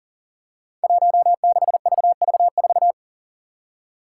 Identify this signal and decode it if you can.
Morse code, heard as a keyed tone, about 40 wpm, 700 Hz. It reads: 16VV4
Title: Five Challenging Letter-Number Combinations 40wpm